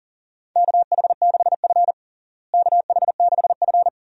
Morse code, heard as a keyed tone, about 40 wpm, 700 Hz. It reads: KH6F KH6F
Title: DXCC-100 Call Signs - 4 Characters - Repeated 2x - 40wpm